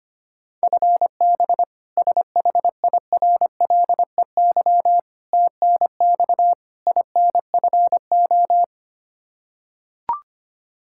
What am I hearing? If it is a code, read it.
Morse code, heard as a keyed tone, about 25 wpm, 700 Hz. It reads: FB SHIRLEY TNX INFO
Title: K1USN SST QSO Practice 25wpm